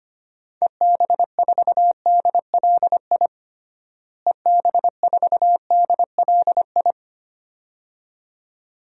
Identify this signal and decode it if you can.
Morse code, heard as a keyed tone, about 25 wpm, 700 Hz. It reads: EB4DLI EB4DLI
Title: DXCC-100 Call Signs - 6 Characters - Repeated 2x - 25wpm